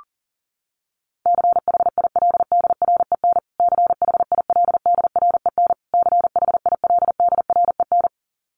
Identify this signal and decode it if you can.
Morse code, heard as a keyed tone, about 40 wpm, 700 Hz. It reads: CHILDREN CHILDREN CHILDREN
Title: Top 300 Words Repeated 40wpm